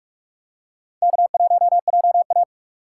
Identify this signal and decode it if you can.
Morse code, heard as a keyed tone, about 43 wpm, 700 Hz. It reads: K1JA
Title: USA Call Signs - 4 Characters - 2x Character Spacing - 45wpm